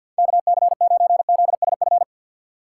Morse code, heard as a keyed tone, about 53 wpm, 700 Hz. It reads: KC9ZSF
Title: USA Call Signs - 6 Characters - 3x Character Spacing - 50wpm